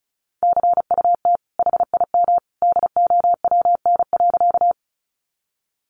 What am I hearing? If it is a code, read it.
Morse code, heard as a keyed tone, about 34 wpm, 700 Hz. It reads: CUT HIM DOWN.
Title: Sentences from Top 300 Words 35wpm